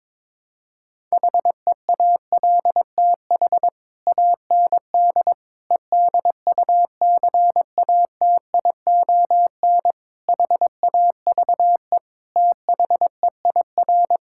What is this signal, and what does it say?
Morse code, heard as a keyed tone, about 22 wpm, 700 Hz. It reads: HEALTH AND EDUCATION HAVE THEIR
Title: Sentences from Top 1000 Words - Review Part 6 - 22wpm